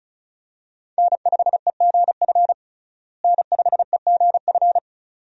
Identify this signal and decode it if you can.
Morse code, heard as a keyed tone, about 34 wpm, 700 Hz. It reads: N5EGF N5EGF
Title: USA Call Signs - 5 Characters - Repeated 2x - 35wpm